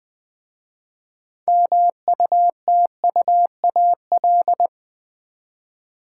Morse code, heard as a keyed tone, about 20 wpm, 700 Hz. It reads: MUTUAL
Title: 6 Letter Words - First 3 Letters Given - 20wpm